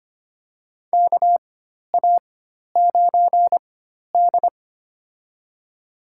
Morse code, heard as a keyed tone, about 25 wpm, spaced as for 12 wpm, 700 Hz. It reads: KA9D
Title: USA Call Signs - 4 Characters - 4x Character Spacing - 25wpm